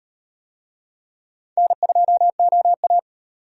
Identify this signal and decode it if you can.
Morse code, heard as a keyed tone, about 35 wpm, 700 Hz. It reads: N2OA